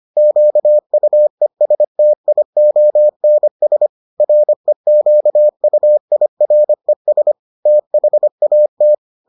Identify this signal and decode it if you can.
Morse code, heard as a keyed tone, about 25 wpm, 600 Hz. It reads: QUESTIONS REQUIRES THAT